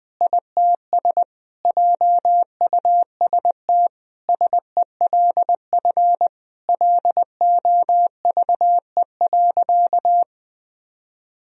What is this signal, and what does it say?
Morse code, heard as a keyed tone, about 20 wpm, 700 Hz. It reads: ITS JUST SELF LOVE.
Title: Sentences from Top 1000 Words - Review Part 1 - 20wpm